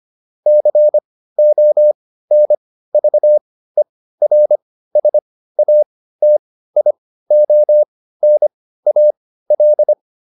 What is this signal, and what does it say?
Morse code, heard as a keyed tone, about 25 wpm, spaced as for 15 wpm, 600 Hz. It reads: CONVERSATIONAL